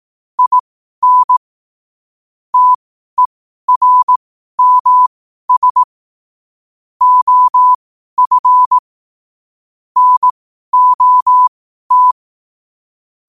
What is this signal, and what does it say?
Morse code, heard as a keyed tone, about 18 wpm, spaced as for 12 wpm, 1000 Hz. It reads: IN TERMS OF NOT
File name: qotdfiles / 12wpm